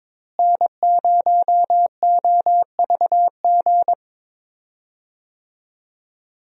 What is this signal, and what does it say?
Morse code, heard as a keyed tone, about 21 wpm, 700 Hz. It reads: N0OVG
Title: USA Call Signs - 5 Characters - Repeated 3x - 22wpm